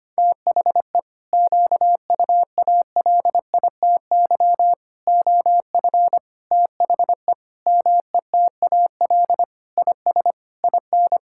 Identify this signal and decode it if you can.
Morse code, heard as a keyed tone, about 25 wpm, 700 Hz. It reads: THE QUALITY OF THE METAL IS IN
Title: Sentences from Top 1000 Words - Review Part 6 - 25wpm